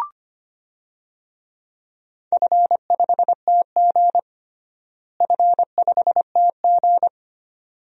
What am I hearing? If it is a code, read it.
Morse code, heard as a keyed tone, about 25 wpm, 700 Hz. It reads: F5TG F5TG
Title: DXCC-100 Call Signs - 4 Characters - Repeated 2x - 25wpm